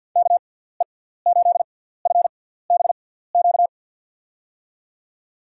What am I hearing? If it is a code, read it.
Morse code, heard as a keyed tone, about 49 wpm, spaced as for 17 wpm, 700 Hz. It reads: KE8FBQ
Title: USA Call Signs - 6x Character Spacing - 50wpm